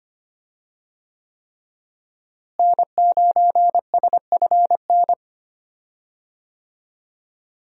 Morse code, heard as a keyed tone, about 25 wpm, 700 Hz. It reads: N9SFN
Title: USA Call Signs - 5 Characters - Repeated 3x - 25wpm